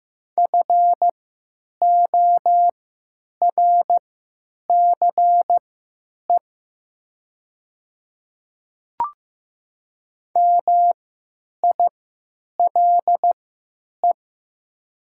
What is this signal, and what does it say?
Morse code, heard as a keyed tone, about 15 wpm, spaced as for 9 wpm, 700 Hz. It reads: FORCE MILE
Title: 3x Character Spacing - Top 500 Words - 15wpm